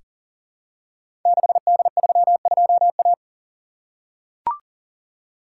Morse code, heard as a keyed tone, about 39 wpm, 690 Hz. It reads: 6D32A
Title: Code Groups - 5 Letter:Numbers - 40wpm